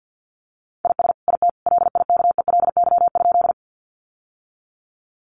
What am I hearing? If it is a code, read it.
Morse code, heard as a keyed tone, about 50 wpm, 700 Hz. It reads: IS IT LIKELY?
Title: Sentences from Top 600 Words 50wpm